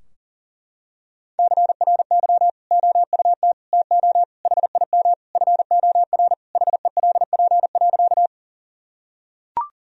Morse code, heard as a keyed tone, about 37 wpm, 700 Hz. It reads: CRY OUT TO HIM FOR HELP.
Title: Sentences from Top 1000 Words - Review Part 2 - 40wpm